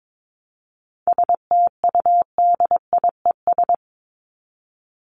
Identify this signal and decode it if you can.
Morse code, heard as a keyed tone, about 22 wpm, 700 Hz. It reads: STUDIES
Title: ES:S Suffix - Top 100 Words with Base Words 22wpm